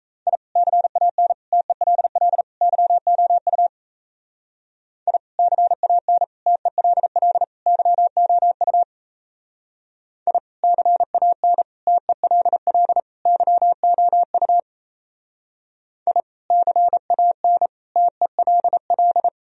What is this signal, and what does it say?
Morse code, heard as a keyed tone, about 34 wpm, 700 Hz. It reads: I CAN TELL YOU I CAN TELL YOU I CAN TELL YOU I CAN TELL
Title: Speed-Racing - Sets of 4 Words 28wpm